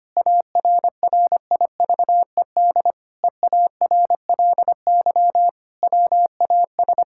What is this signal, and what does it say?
Morse code, heard as a keyed tone, about 25 wpm, 700 Hz. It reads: ARRIVED EARLY WAS
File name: practice / 25wpm / Sentences with 4-Word Phrases - 25wpm